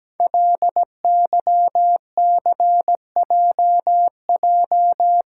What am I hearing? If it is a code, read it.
Morse code, heard as a keyed tone, about 17 wpm, 700 Hz. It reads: LYCJJ
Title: Code Groups - 5 Letter:Numbers - 17wpm